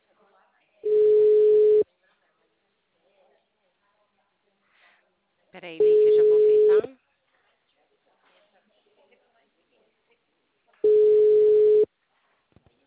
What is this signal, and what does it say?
Ringback tone (Europe, 425 Hz), with speech.
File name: out- -2007- - - .2933